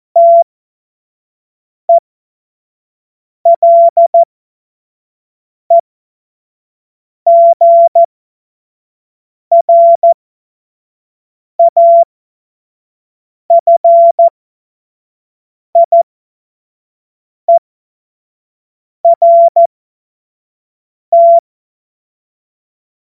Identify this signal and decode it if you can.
Morse code, heard as a keyed tone, about 14 wpm, spaced as for 5 wpm, 680 Hz. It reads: TELEGRAFIERT
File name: Morsekurs / Witze / Witz 21. Juni 2022 25 F70